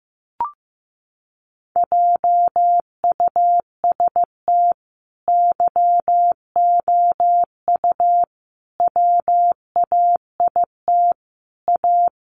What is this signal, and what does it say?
Morse code, heard as a keyed tone, about 15 wpm, 700 Hz. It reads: JUST YOU WAIT A
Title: Sentences from Top 900 Words 15wpm